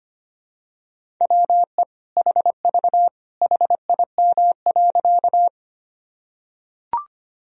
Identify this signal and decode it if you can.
Morse code, heard as a keyed tone, about 25 wpm, 700 Hz. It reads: WE HV HIM.
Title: Sentences from Top 100 Words and Abbreviations 25wpm